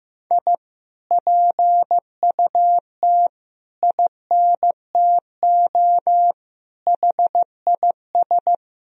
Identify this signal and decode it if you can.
Morse code, heard as a keyed tone, about 16 wpm, 700 Hz. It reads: I PUT INTO HIS
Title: Sentences from Top 1000 Words - Review Part 2 - 15wpm